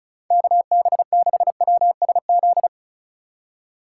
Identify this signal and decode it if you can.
Morse code, heard as a keyed tone, about 36 wpm, 690 Hz. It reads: KB6WSZ
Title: USA Call Signs - 6 Characters - 4x Character Spacing - 35wpm